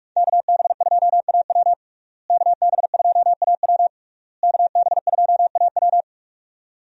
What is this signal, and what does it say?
Morse code, heard as a keyed tone, about 41 wpm, 700 Hz. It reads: KB2AW KB2AW KB2AW
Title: USA Call Signs - 5 Characters - Repeated 3x - 45wpm